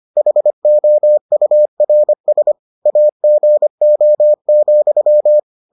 Morse code, heard as a keyed tone, about 25 wpm, 600 Hz. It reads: HOURS AGO,